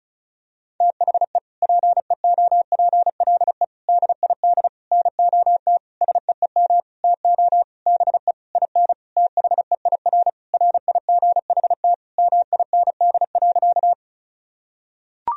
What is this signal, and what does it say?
Morse code, heard as a keyed tone, about 36 wpm, 700 Hz. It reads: THE PEOPLE DID NOT SEEM TO BE IN THEIR RIGHT MIND.
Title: Sentences from Top 1000 Words - Review Part 1 - 35wpm